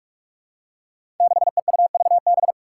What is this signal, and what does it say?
Morse code, heard as a keyed tone, about 46 wpm, 700 Hz. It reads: 6EUVB
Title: Five Challenging Letter-Number Combinations 45wpm